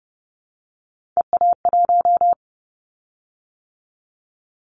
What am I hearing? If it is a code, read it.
Morse code, heard as a keyed tone, about 30 wpm, 700 Hz. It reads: EA1
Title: Callsign Prefixes for Top 25 DXCC - 30wpm